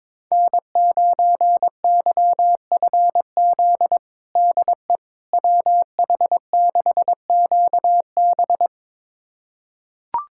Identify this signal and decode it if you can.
Morse code, heard as a keyed tone, about 22 wpm, 700 Hz. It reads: N9YFZ DE WH6QB
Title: USA Callsign DE Callsign - 5 Characters - 22wpm